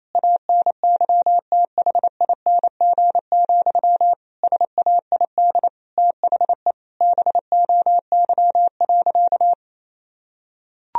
Morse code, heard as a keyed tone, about 28 wpm, 700 Hz. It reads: ANYTHING, SAID THE BOY.
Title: Sentences from Top 1000 Words - Review Part 4 - 28wpm